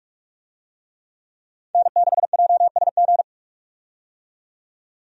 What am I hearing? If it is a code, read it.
Morse code, heard as a keyed tone, about 46 wpm, 700 Hz. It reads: N6JSG
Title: USA Call Signs - 5 Characters - 45wpm